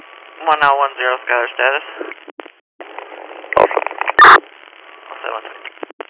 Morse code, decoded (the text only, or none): none